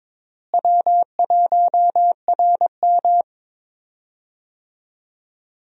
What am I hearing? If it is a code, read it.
Morse code, heard as a keyed tone, about 22 wpm, 700 Hz. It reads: W1RM
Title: USA Call Signs - 4 Characters - Repeated 3x - 22wpm